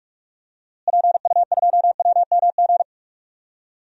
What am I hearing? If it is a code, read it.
Morse code, heard as a keyed tone, about 46 wpm, 700 Hz. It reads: PU2WMG
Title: DXCC-100 Call Signs - 6 Characters - Repeated 2x - 45wpm